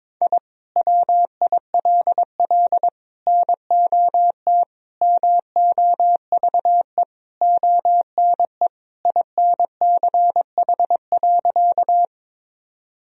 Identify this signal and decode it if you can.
Morse code, heard as a keyed tone, about 22 wpm, 700 Hz. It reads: I WILL NOT MOVE ONE INCH.
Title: Sentences from Top 1000 Words - Review Part 3 - 22wpm